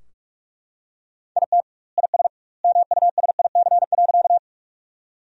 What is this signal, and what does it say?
Morse code, heard as a keyed tone, about 43 wpm, 700 Hz. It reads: IT IS MUSIC.
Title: 2x Word-Spacing - Sentences from Top 300 Words - 45wpm